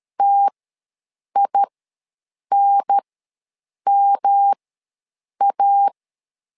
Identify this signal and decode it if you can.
Morse code, heard as a keyed tone, about 13 wpm, spaced as for 7 wpm, 800 Hz. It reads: TINMA
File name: LEZIONI / 3 MWRO / 3 MW ws 12wpm 1816